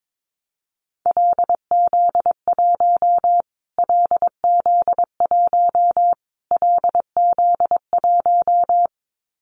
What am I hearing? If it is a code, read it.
Morse code, heard as a keyed tone, about 22 wpm, 700 Hz. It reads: LZ1 LZ1 LZ1
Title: Callsign Prefixes for Top 25 DXCC - Repeated 3x - 22wpm